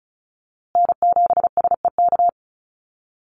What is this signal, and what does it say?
Morse code, heard as a keyed tone, about 34 wpm, 700 Hz. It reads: N7SEK
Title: USA Call Signs Repeated 2x 35wpm